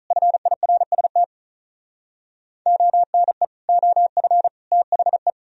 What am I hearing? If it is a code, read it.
Morse code, heard as a keyed tone, about 46 wpm, 700 Hz. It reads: FIRST ONE OF THE